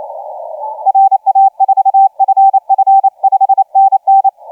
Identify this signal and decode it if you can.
Morse code, heard as a keyed tone, about 28 wpm, 780 Hz, with radio noise. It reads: RA4FF 5NN
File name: JW LY2KW 1502UTC 15m